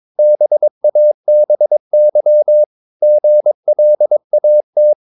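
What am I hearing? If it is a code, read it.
Morse code, heard as a keyed tone, about 22 wpm, 600 Hz. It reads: BABY GLAT
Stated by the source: Morse practice of short Danish words at 22 wpm, from CW med Gnister.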